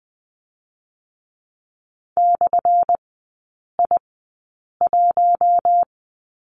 Morse code, heard as a keyed tone, about 20 wpm, spaced as for 8 wpm, 700 Hz. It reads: /I1
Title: 178 Lesson 29e - Slash - Three Characters 20wpm